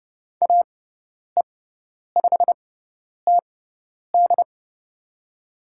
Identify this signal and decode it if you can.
Morse code, heard as a keyed tone, about 30 wpm, spaced as for 10 wpm, 700 Hz. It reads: AE5TD
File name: Lesson 17g - Letter u - Call Signs 30wpm